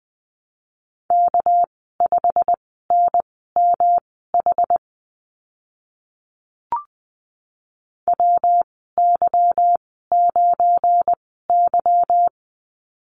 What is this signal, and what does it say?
Morse code, heard as a keyed tone, about 20 wpm, spaced as for 14 wpm, 700 Hz. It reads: K5NMH WY9Y
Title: 207 Lesson 33g - Letter j - Call Signs 20wpm